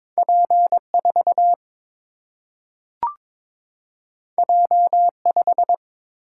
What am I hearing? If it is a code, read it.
Morse code, heard as a keyed tone, about 22 wpm, 700 Hz. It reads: P 4 J 5
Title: Single-Letter Single-Number 22wpm